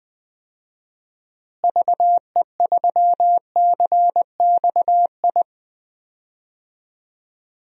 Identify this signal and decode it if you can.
Morse code, heard as a keyed tone, about 20 wpm, 700 Hz. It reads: VE3CXI
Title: DXCC-100 Call Signs - 6 Characters - Repeated 3x - 20wpm